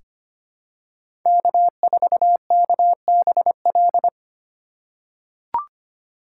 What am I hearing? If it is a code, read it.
Morse code, heard as a keyed tone, about 25 wpm, 700 Hz. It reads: K4KBL
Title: K1USN SST QSO Practice - 25wpm